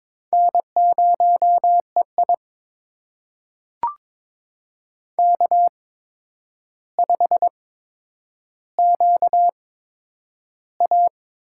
Morse code, heard as a keyed tone, about 22 wpm, spaced as for 9 wpm, 700 Hz. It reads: N0EI K5QA
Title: USA Call Signs - 4 Characters - 8x Character Spacing - 22wpm